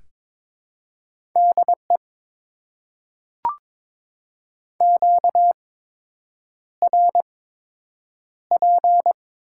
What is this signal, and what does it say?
Morse code, heard as a keyed tone, about 22 wpm, spaced as for 8 wpm, 700 Hz. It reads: DE QRP
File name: Top 90 QSO Elements - 8x Character Spacing - 22wpm